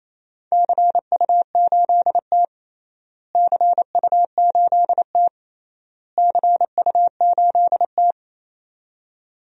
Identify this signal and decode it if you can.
Morse code, heard as a keyed tone, about 28 wpm, 700 Hz. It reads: CU8T CU8T CU8T
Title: DXCC-100 Call Signs - 4 Characters - Repeated 3x - 28wpm